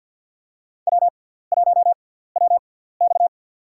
Morse code, heard as a keyed tone, about 53 wpm, spaced as for 17 wpm, 700 Hz. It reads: W1WX
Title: USA Call Signs - 4 Characters - 6x Character Spacing - 50wpm